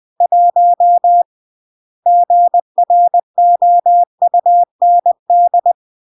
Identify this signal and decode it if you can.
Morse code, heard as a keyed tone, about 20 wpm, 700 Hz. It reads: 1 GROUND